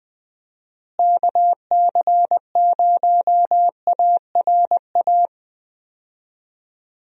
Morse code, heard as a keyed tone, about 20 wpm, 700 Hz. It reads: KC0ARA
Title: USA Call Signs - 6 Characters - Repeated 2x - 20wpm